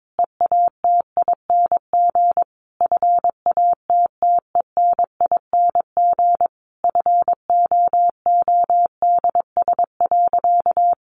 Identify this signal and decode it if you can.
Morse code, heard as a keyed tone, about 22 wpm, 700 Hz. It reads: EATING FATTENING FOODS.
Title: Easy Phrases 22wpm